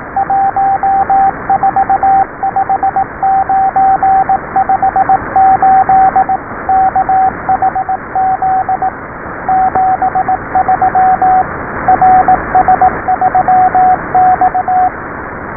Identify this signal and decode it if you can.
Morse code, heard as a keyed tone, about 18 wpm, 780 Hz, with radio noise. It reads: 145958 KHZ 73 RS3X